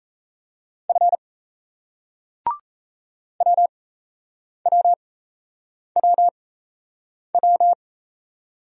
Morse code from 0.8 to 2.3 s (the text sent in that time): F